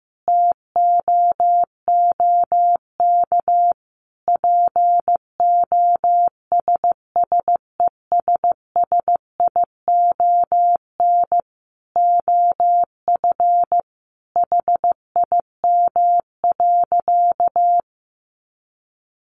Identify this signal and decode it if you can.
Morse code, heard as a keyed tone, about 15 wpm, 700 Hz. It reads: TOOK POSSESSION OF HIM.
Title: Sentences from Top 1400 Words 15wpm